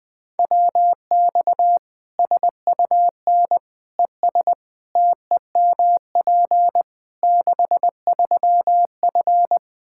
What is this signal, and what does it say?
Morse code, heard as a keyed tone, about 20 wpm, 700 Hz. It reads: WX SUN ES TEMP 63F
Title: Practice QSO - Exchange 2 - Info - 20wpm